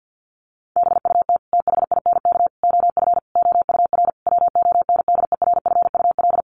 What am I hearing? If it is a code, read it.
Morse code, heard as a keyed tone, about 50 wpm, 700 Hz. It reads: BUT THINK OF OUR WONDERFUL
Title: Sentences from Top 1100 Words 50wpm